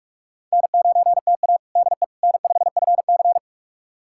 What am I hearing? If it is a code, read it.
Morse code, heard as a keyed tone, about 46 wpm, 700 Hz. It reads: N9TA DE N5FC
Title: USA Callsign DE Callsign - 4 Characters - 45wpm